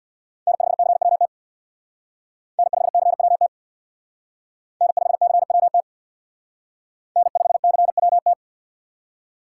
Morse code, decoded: N5CWT N5CWT N5CWT N5CWT

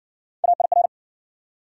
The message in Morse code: AIR

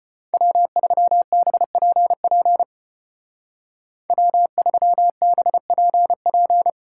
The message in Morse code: W3BPP W3BPP

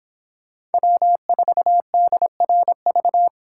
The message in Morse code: W4DRV

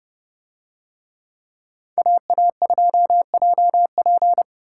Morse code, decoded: AA2JP

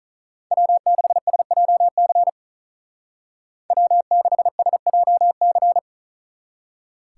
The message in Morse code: W6SJC W6SJC

